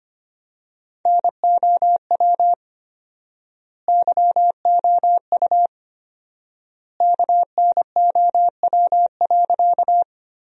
Morse code, heard as a keyed tone, about 25 wpm, 700 Hz. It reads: NOW YOU KNOW.
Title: 4x Word-Spacing - Sentences from Top 100 Words - 25wpm